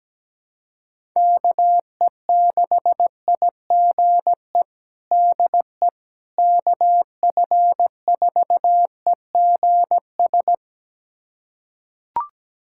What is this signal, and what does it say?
Morse code, heard as a keyed tone, about 17 wpm, 700 Hz. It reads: KE6IGE DE KF4EGS
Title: USA Callsign DE Callsign - 6 Characters - 17wpm